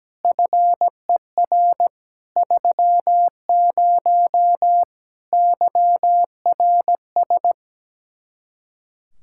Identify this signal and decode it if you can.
Morse code, heard as a keyed tone, about 17 wpm, 700 Hz. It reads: FER 30 YRS
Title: Practice QSO - Exchange 3 - More Info - 17wpm